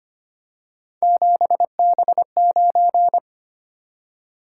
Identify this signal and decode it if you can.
Morse code, heard as a keyed tone, about 25 wpm, 700 Hz. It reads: 7B9
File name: Three Letter-Number Combinations - 25wpm